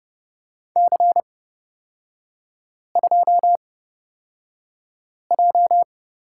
Morse code, heard as a keyed tone, about 30 wpm, spaced as for 5 wpm, 700 Hz. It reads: C2J
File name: Lesson 36e - BT - Three Characters 30wpm